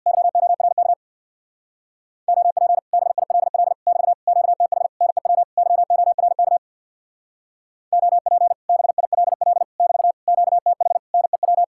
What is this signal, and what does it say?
Morse code, heard as a keyed tone, about 62 wpm, 700 Hz. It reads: YORK OP BILL <BT> QTH NEW YORK OP BILL <BT> QTH NEW
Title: Speed-Racing Common QSO Phrases - 50wpm